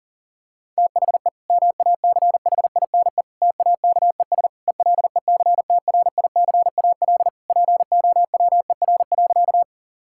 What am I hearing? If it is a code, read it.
Morse code, heard as a keyed tone, about 40 wpm, 700 Hz. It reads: THE MACHINE TAKES ELECTRICAL POWER.
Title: Sentences from Top 1000 Words - Review Part 6 - 40wpm